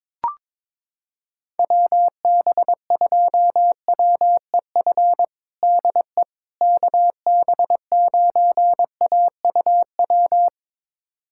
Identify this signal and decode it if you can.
Morse code, heard as a keyed tone, about 22 wpm, 700 Hz. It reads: WB2WEF DE KB9AUW
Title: USA Callsign DE Callsign 22wpm